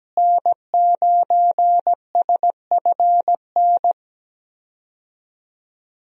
Morse code, heard as a keyed tone, about 17 wpm, 700 Hz. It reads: N9SFN
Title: USA Call Signs - 5 Characters - Repeated 3x - 17wpm